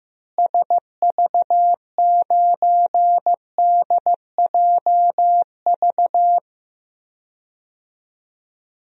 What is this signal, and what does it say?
Morse code, heard as a keyed tone, about 15 wpm, 700 Hz. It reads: SV9DJV
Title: DXCC-100 Call Signs - 6 Characters - Repeated 2x - 15wpm